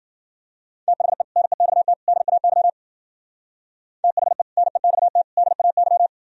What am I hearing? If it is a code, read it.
Morse code, heard as a keyed tone, about 51 wpm, 700 Hz. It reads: THE NEXT DAY THE NEXT DAY